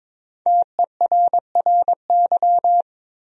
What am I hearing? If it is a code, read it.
Morse code, heard as a keyed tone, about 21 wpm, 700 Hz. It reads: TERRY
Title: USA Common Names - 22wpm